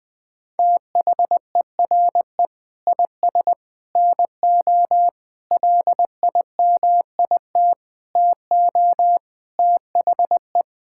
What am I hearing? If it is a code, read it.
Morse code, heard as a keyed tone, about 21 wpm, 700 Hz. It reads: THERE IS NO LIMIT TO THE
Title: Sentences with 8-Word Phrases - Spoken Beginning - 20wpm